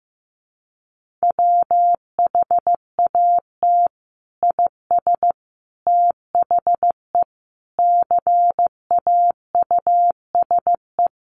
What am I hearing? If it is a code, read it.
Morse code, heard as a keyed tone, about 14 wpm, 700 Hz. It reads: WHAT IS THE CAUSE
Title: Sentences from Top 1100 Words 15wpm